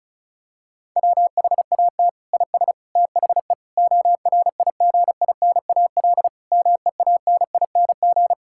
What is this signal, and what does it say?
Morse code, heard as a keyed tone, about 36 wpm, 700 Hz. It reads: WHAT IS THE ORIGINAL MEANING
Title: Sentences from Top 1000 Words - Review Part 6 - 35wpm